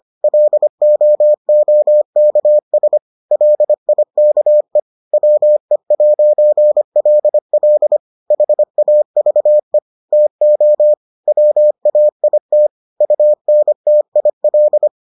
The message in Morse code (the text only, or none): LOOKS LIKE WE'LL HAVE TO WAIT UNTIL